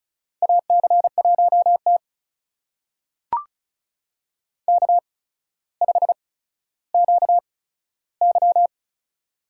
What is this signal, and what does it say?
Morse code, heard as a keyed tone, about 36 wpm, spaced as for 15 wpm, 700 Hz. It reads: AC1T K5QY
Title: USA Call Signs - 4 Characters - 8x Character Spacing - 35wpm